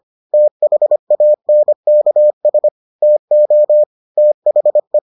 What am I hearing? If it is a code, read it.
Morse code, heard as a keyed tone, about 25 wpm, 600 Hz. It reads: THANKS TO THE